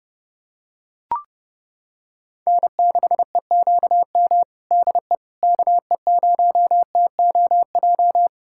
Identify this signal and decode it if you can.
Morse code, heard as a keyed tone, about 29 wpm, 700 Hz. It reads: N6EQM DE KE0TOJ
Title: USA Callsign DE Callsign 30wpm